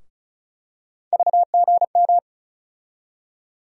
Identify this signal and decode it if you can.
Morse code, heard as a keyed tone, about 36 wpm, 700 Hz. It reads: VGM
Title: USA Call Sign Postfixes - 3 Letters 35wpm